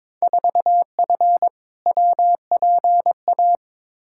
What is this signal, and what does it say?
Morse code, heard as a keyed tone, about 21 wpm, 700 Hz. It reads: 4F WPA
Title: Field Day Exchanges - 22wpm